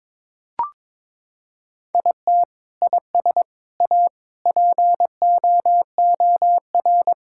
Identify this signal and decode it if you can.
Morse code, heard as a keyed tone, about 22 wpm, 700 Hz. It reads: IT IS A POOR
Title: Sentences from Top 1000 Words - Review Part 5 - 22wpm